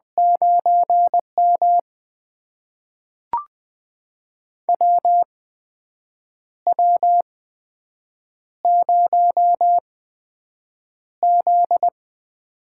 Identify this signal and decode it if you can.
Morse code, heard as a keyed tone, about 20 wpm, spaced as for 7 wpm, 700 Hz. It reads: 9M WW0Z
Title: USA Call Signs - 4 Characters - 8x Character Spacing - 20wpm